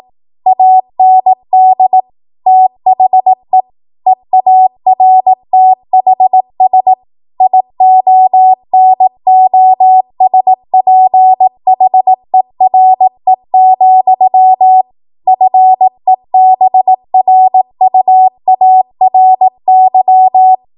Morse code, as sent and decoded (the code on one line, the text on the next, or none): .- -. -.. / - .... / . .- .-. - .... ... / .. --- -. --- ... .--. .... .-. --..-- / ..-. -... .-. ..- .- .-. -.--
AND THE EARTHS IONOSPHERE, FEBRUARY